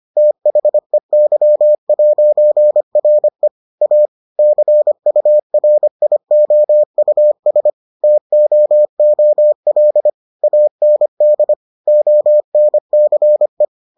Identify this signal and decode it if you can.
Morse code, heard as a keyed tone, about 25 wpm, 600 Hz. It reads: THEY'RE A CURIOUS TOOL AND ONCE